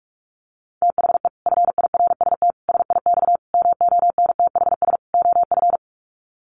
Morse code, heard as a keyed tone, about 44 wpm, 700 Hz. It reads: THE FIRST SIX MONTHS OF
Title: Sets of 5 Words Encore2 45wpm